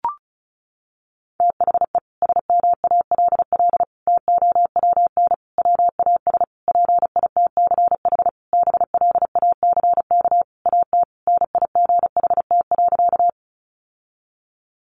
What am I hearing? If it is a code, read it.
Morse code, heard as a keyed tone, about 34 wpm, 700 Hz. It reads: THE SMALL TOWN WAS PITCH BLACK AT NIGHT.
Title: Sentences from Top 1300 Words 35wpm